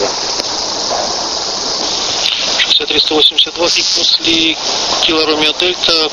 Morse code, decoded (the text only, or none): none